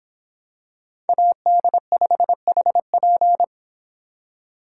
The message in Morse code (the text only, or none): AD5HP